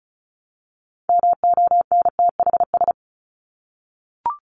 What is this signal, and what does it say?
Morse code, heard as a keyed tone, about 34 wpm, 700 Hz. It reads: MONTHS
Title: ES:S Suffix - Top 100 Words with Base Words 35wpm